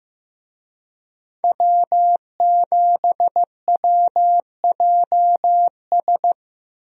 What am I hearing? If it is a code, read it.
Morse code, heard as a keyed tone, about 16 wpm, 700 Hz. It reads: W7WJS
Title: USA Call Signs - 5 Characters - Repeated 3x - 15wpm